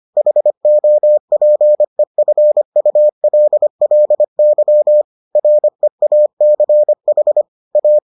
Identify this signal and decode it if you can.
Morse code, heard as a keyed tone, about 25 wpm, 600 Hz. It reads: HOPEFULLY REACH A